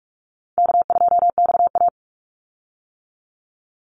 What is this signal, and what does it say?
Morse code, heard as a keyed tone, about 44 wpm, 700 Hz. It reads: K2XA
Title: Call Signs - 45wpm